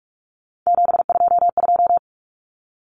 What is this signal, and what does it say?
Morse code, heard as a keyed tone, about 44 wpm, 700 Hz. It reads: 722
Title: Common Cut Numbers - 3-Digit 45wpm